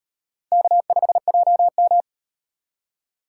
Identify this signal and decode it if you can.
Morse code, heard as a keyed tone, about 38 wpm, 700 Hz. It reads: K5JM